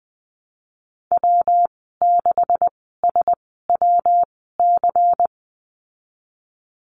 Morse code, heard as a keyed tone, about 20 wpm, spaced as for 15 wpm, 700 Hz. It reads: W6SWC
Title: 173 Lesson 28g - Number 9 - Call Signs 20wpm